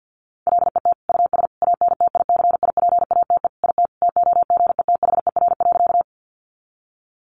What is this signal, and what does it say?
Morse code, heard as a keyed tone, about 50 wpm, 700 Hz. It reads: LET US ANTICIPATE IT TOGETHER.
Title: Sentences from Top 1500 Words 50wpm